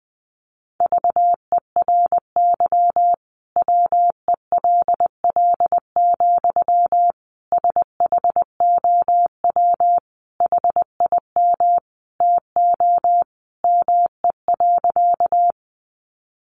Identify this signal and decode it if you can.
Morse code, heard as a keyed tone, about 20 wpm, 700 Hz. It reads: VERY WELL, SHOW HIM TO ME.
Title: Sentences from Top 200 Words 20wpm